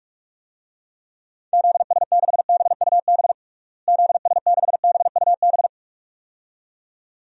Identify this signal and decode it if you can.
Morse code, heard as a keyed tone, about 46 wpm, 700 Hz. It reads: ZS6BUB ZS6BUB
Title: DXCC-100 Call Signs - 6 Characters - Repeated 2x - 45wpm